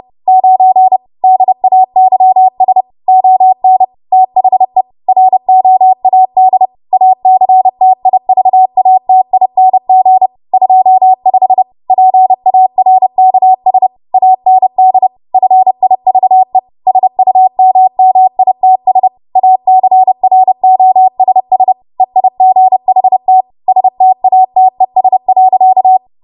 Morse code, heard as a keyed tone, about 30 wpm, 750 Hz. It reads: 9 DAYS ON THE ROAD ACTIVATING 25 PARKS AND FIVE SUMMITS ACROSS EIGHT STATES.
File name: Morse / 30WPM